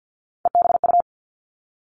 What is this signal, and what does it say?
Morse code, heard as a keyed tone, about 50 wpm, 700 Hz. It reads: EBU
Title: Three Challenging Letter-Number Combinations 50wpm